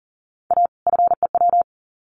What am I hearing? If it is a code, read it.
Morse code, heard as a keyed tone, about 40 wpm, 700 Hz. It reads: A FEW
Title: Sets of 2 Words 40wpm